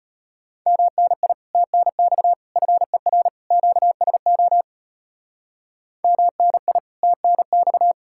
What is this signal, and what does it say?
Morse code, heard as a keyed tone, about 32 wpm, 700 Hz. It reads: MNI TNX FER QSO MNI TNX